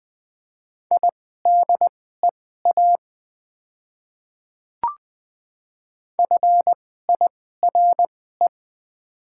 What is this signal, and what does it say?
Morse code, heard as a keyed tone, about 20 wpm, spaced as for 14 wpm, 700 Hz. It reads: IDEA FIRE
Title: 2x Character Spacing - Top 500 Words - 20wpm